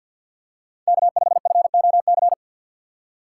Morse code, heard as a keyed tone, about 48 wpm, 700 Hz. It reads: K5FOC
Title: USA Call Signs - 5 Characters - 4x Character Spacing - 50wpm